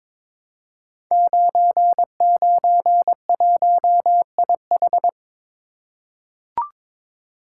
Morse code, heard as a keyed tone, about 22 wpm, 700 Hz. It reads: 991IH
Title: Code Groups - 5 Letter:Numbers - 22wpm